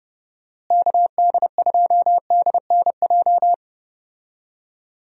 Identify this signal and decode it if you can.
Morse code, heard as a keyed tone, about 30 wpm, 700 Hz. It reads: KD2DNJ
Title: USA Call Signs - 6 Characters - Repeated 2x - 30wpm